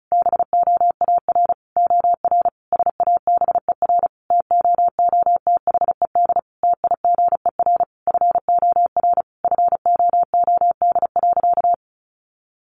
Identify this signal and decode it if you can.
Morse code, heard as a keyed tone, about 34 wpm, 700 Hz. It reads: BOAR OR SABER TOOTHED TIGER FOR FOOD.
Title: A Sound Of Thunder 35wpm